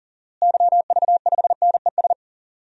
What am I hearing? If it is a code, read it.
Morse code, heard as a keyed tone, about 40 wpm, 700 Hz. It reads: YV5NES
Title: DXCC-100 Call Signs - 6 Characters - Repeated 2x - 40wpm